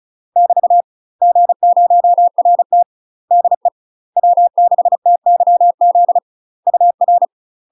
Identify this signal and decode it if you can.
Morse code, heard as a keyed tone, about 35 wpm, 700 Hz. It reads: <BT> G0RT DE W6TYZ UR
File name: UK MASTER QSOs TRAINING / 35WPM / UK MASTER QSOs 0003